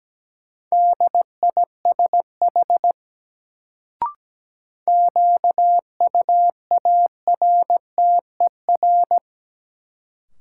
Morse code, heard as a keyed tone, about 17 wpm, 700 Hz. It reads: DISH QUARTER
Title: Sentences from Top 1300 Words - New Word Drill - 17wpm